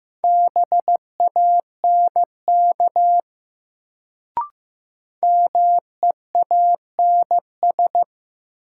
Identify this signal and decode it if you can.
Morse code, heard as a keyed tone, about 15 wpm, 700 Hz. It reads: BANK MEANS
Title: Sets of 5 Words - New Word Drill - 15wpm